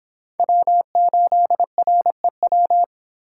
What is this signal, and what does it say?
Morse code, heard as a keyed tone, about 30 wpm, 700 Hz. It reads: W8REW